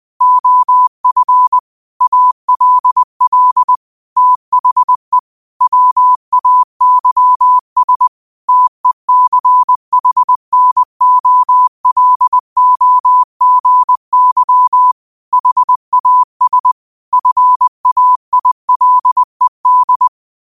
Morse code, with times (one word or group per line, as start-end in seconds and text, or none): 0.2-1.6 OF
2.0-3.8 ALL
4.2-5.2 THE
5.6-8.1 WAYS
8.5-14.9 TECHNOLOGY
15.3-16.7 HAS
17.1-20.1 FAILED